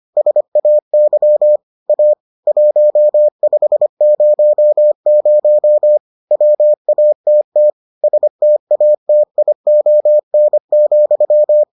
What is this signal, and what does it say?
Morse code, heard as a keyed tone, about 25 wpm, 600 Hz. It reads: SAY A 1500 WATT STATION,